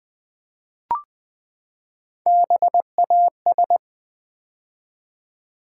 Morse code, heard as a keyed tone, about 20 wpm, 700 Hz. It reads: BAS
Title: Build A Word - 3 Letters At A Time - 8-Letter Words - 20wpm